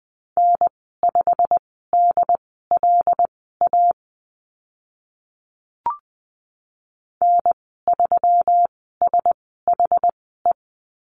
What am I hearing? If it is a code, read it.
Morse code, heard as a keyed tone, about 20 wpm, spaced as for 15 wpm, 700 Hz. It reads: N5DLA N3SHE
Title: 140 Lesson 23g - Letter f - Call Signs 20wpm